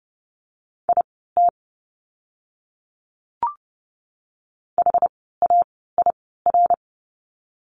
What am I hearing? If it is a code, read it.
Morse code, heard as a keyed tone, about 30 wpm, spaced as for 17 wpm, 700 Hz. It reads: IT HAIR
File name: Lesson 17f - Letter u - Words 30wpm